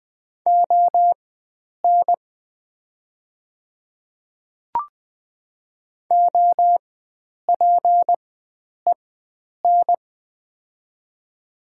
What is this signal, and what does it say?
Morse code, heard as a keyed tone, about 20 wpm, spaced as for 9 wpm, 700 Hz. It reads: ON OPEN
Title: 4x Character Spacing - Top 500 Words - 20wpm